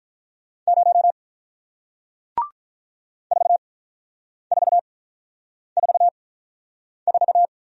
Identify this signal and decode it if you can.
Morse code, heard as a keyed tone, about 46 wpm, spaced as for 9 wpm, 700 Hz. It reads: 0 4444